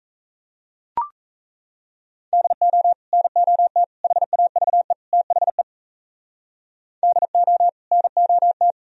Morse code, heard as a keyed tone, about 39 wpm, 700 Hz. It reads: DO NOT HAVE THE DO NOT